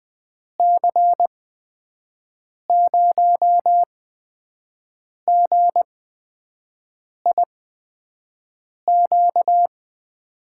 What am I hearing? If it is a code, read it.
Morse code, heard as a keyed tone, about 20 wpm, spaced as for 5 wpm, 700 Hz. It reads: C0GIQ
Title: USA Call Signs - 6 Characters - 8x Character Spacing - 20wpm